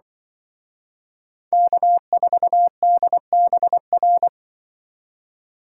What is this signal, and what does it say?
Morse code, heard as a keyed tone, about 23 wpm, 700 Hz. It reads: K4DBR
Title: Speed Racing - USA 1x3 Call Signs 20wpm